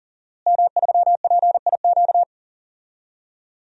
Morse code, heard as a keyed tone, about 40 wpm, 700 Hz. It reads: M3PIQ
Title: DXCC-100 Call Signs - 5 Characters - Repeated 2x - 40wpm